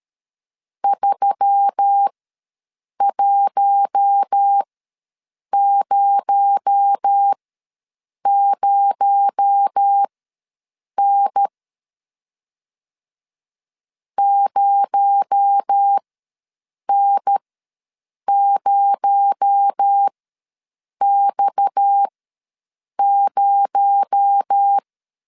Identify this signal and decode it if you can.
Morse code, heard as a keyed tone, about 13 wpm, spaced as for 7 wpm, 800 Hz. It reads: 3100N 0N0X0
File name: LEZIONI / 6 PKX0 / 6 PKX0 ws 12wpm 3448